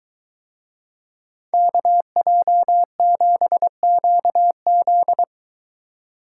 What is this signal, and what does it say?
Morse code, heard as a keyed tone, about 23 wpm, 700 Hz. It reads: KJ7QZ